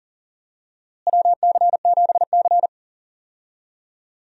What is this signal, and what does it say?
Morse code, heard as a keyed tone, about 40 wpm, 700 Hz. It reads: WC7C
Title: USA Call Signs - 4 Characters - Repeated 3x - 40wpm